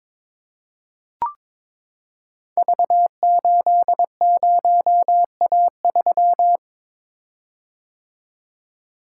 Morse code, heard as a keyed tone, about 22 wpm, 700 Hz. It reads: V80A3
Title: Code Groups - 5 Letter:Numbers - 22wpm